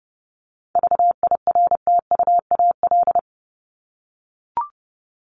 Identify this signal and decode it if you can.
Morse code, heard as a keyed tone, about 30 wpm, 700 Hz. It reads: VIRTUAL
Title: LY Suffix - Top 100 Words with Base Words 30wpm